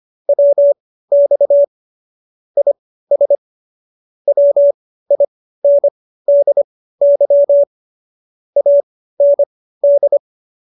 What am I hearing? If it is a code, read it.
Morse code, heard as a keyed tone, about 25 wpm, spaced as for 15 wpm, 570 Hz. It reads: WX IS WINDY AND
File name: qso222 15